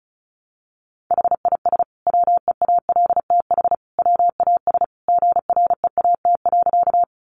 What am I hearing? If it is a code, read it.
Morse code, heard as a keyed tone, about 34 wpm, 700 Hz. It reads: HIS WEALTH WAS GREAT.
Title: Sentences from Top 1200 Words 35wpm